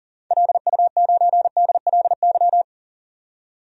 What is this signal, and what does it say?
Morse code, heard as a keyed tone, about 40 wpm, 700 Hz. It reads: LU9DLY
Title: DXCC-100 Call Signs - 6 Characters - 40wpm